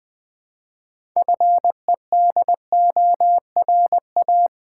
Morse code, heard as a keyed tone, about 20 wpm, 700 Hz. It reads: FEDORA
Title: Difficult Words with Hints - 20wpm